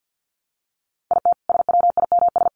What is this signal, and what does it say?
Morse code, heard as a keyed tone, about 48 wpm, 700 Hz. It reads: IT SWIMS
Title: Easy Phrases 50wpm